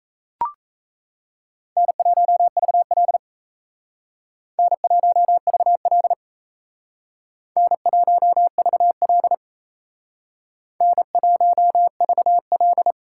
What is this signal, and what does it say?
Morse code, heard as a keyed tone, about 34 wpm, 700 Hz. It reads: N1VL N1VL N1VL N1VL
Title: Speed Racing - USA 1x2 Call Signs 28wpm